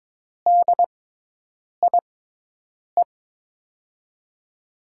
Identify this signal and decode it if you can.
Morse code, heard as a keyed tone, about 22 wpm, spaced as for 8 wpm, 700 Hz. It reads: DIE
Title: USA Call Signs - 6 Characters - 6x Character Spacing - 22wpm